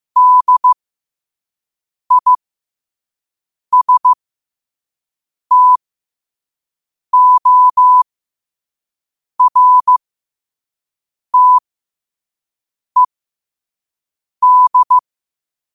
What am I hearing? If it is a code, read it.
Morse code, heard as a keyed tone, about 15 wpm, spaced as for 5 wpm, 1000 Hz. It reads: DISTORTED
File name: qotdfiles / 5wpm